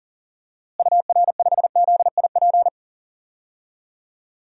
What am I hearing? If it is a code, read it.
Morse code, heard as a keyed tone, about 40 wpm, 700 Hz. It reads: UR5ZIP
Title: DXCC-100 Call Signs - 6 Characters - Repeated 3x - 40wpm